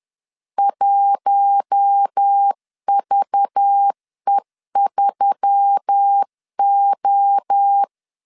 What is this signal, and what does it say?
Morse code, heard as a keyed tone, about 10 wpm, 800 Hz. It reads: 1VE3O
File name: LEZIONI / 7 GF9Z / 7 G 10wpm 2501